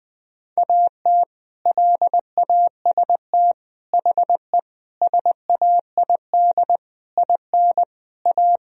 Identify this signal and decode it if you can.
Morse code, heard as a keyed tone, about 21 wpm, 700 Hz. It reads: AT LAST HE SAID IN A
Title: Sentences from Top 1000 Words - Review Part 1 - 20wpm